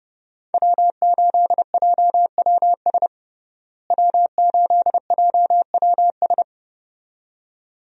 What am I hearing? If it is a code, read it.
Morse code, heard as a keyed tone, about 30 wpm, 700 Hz. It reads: W8JWS W8JWS
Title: USA Call Signs - 5 Characters - Repeated 2x - 30wpm